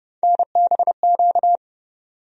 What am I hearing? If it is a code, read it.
Morse code, heard as a keyed tone, about 30 wpm, 700 Hz. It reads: NBQ
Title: USA Call Sign Postfixes - 3 Letters 30wpm